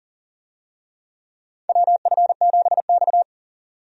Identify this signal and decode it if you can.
Morse code, heard as a keyed tone, about 39 wpm, 700 Hz. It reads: WF7X
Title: USA Call Signs - 4 Characters - 40wpm